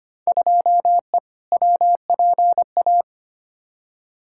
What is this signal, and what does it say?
Morse code, heard as a keyed tone, about 25 wpm, 700 Hz. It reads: 2E WPA
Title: Field Day Exchanges - 25wpm